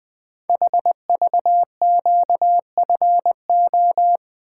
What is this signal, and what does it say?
Morse code, heard as a keyed tone, about 20 wpm, 700 Hz. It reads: HVQFO
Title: Code Groups - 5 Letters - 20wpm